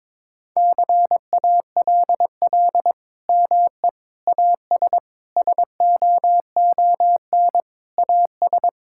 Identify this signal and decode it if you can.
Morse code, heard as a keyed tone, about 22 wpm, 700 Hz. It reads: CALL ME AS SOON AS
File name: practice / 22wpm / Sentences with 3-Word Phrases - 22wpm